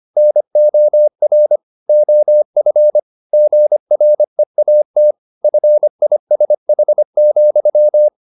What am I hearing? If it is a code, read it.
Morse code, heard as a keyed tone, about 25 wpm, 600 Hz. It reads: NOR OF GREAT FISH,
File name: the old man and the sea part 1 25wpm